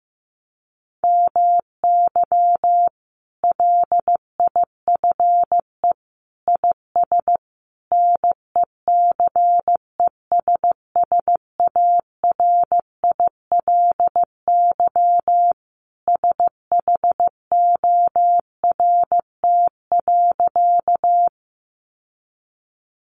Morse code, heard as a keyed tone, about 15 wpm, 700 Hz. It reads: MY LIFE IS NECESSARILY SHORT.
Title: Sentences from Top 1400 Words 15wpm